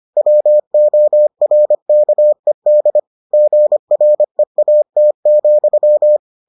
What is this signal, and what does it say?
Morse code, heard as a keyed tone, about 25 wpm, 600 Hz. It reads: WORKED GREAT,